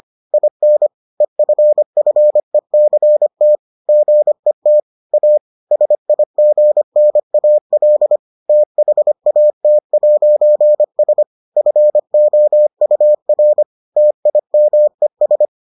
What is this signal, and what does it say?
Morse code, heard as a keyed tone, about 25 wpm, 600 Hz. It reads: IN EFFECT GET A SIGNAL THAT'S FOUR TIMES